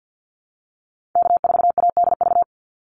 Morse code, heard as a keyed tone, about 50 wpm, 700 Hz. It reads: K4ADV
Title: NA SOTA Activator Callsigns 50wpm